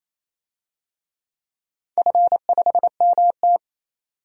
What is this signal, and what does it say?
Morse code, heard as a keyed tone, about 28 wpm, 700 Hz. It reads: F5MT
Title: DXCC-100 Call Signs - 4 Characters - 28wpm